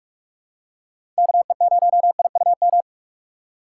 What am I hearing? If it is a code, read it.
Morse code, heard as a keyed tone, about 46 wpm, 700 Hz. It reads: KE0IUM
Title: USA Call Signs - 6 Characters - 3x Character Spacing - 45wpm